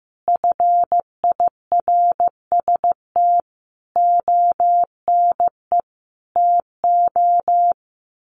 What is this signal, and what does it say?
Morse code, heard as a keyed tone, about 15 wpm, 700 Hz. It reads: FIRST ONE TO
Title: Sentences from Top 1300 Words 15wpm